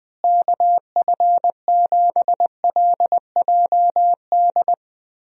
Morse code, heard as a keyed tone, about 20 wpm, 700 Hz. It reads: KF7LJD
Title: USA Call Signs - 6 Characters - Repeated 2x - 20wpm